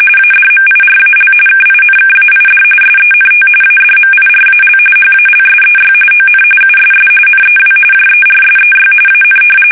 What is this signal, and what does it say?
RTTY, about 45.45 baud. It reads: ADMIRAL KINKAID SENT ADMIRAL HALSEY ON USS NEW JERSEY AN
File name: fsk850 / battleshipnj